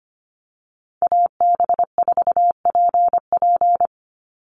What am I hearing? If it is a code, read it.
Morse code, heard as a keyed tone, about 25 wpm, 700 Hz. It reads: AB4PP
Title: NA SOTA Chaser Callsigns 25wpm